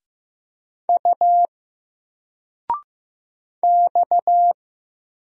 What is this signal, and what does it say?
Morse code, heard as a keyed tone, about 15 wpm, 700 Hz. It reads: U X
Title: Single Letter-Number 15wpm